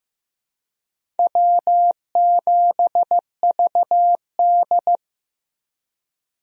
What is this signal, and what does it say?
Morse code, heard as a keyed tone, about 15 wpm, 700 Hz. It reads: W7VD
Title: USA Call Signs - 4 Characters - Repeated 2x - 15wpm